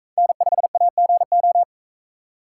Morse code, heard as a keyed tone, about 40 wpm, 700 Hz. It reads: N5AGO